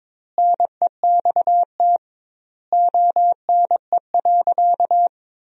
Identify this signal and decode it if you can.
Morse code, heard as a keyed tone, about 22 wpm, 700 Hz. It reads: NEXT ONE.
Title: 2x Word-Spacing - Sentences from Top 300 Words - 22wpm